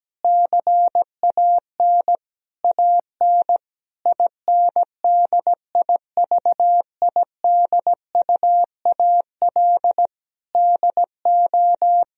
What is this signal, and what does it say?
Morse code, heard as a keyed tone, about 17 wpm, 700 Hz. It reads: CAN AN INDIVIDUAL DO
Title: Sentences from Top 1000 Words - Review Part 4 - 17wpm